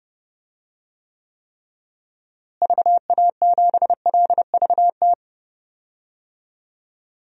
Morse code, heard as a keyed tone, about 30 wpm, 700 Hz. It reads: VA7LVT
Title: DXCC-100 Call Signs - 6 Characters - Repeated 3x - 30wpm